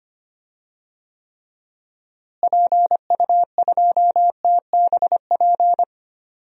Morse code, heard as a keyed tone, about 25 wpm, 700 Hz. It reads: PU2TBP
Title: DXCC-100 Call Signs - 6 Characters - Repeated 2x - 25wpm